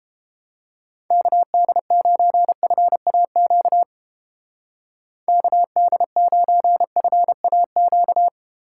Morse code, heard as a keyed tone, about 29 wpm, 700 Hz. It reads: KD9FAQ KD9FAQ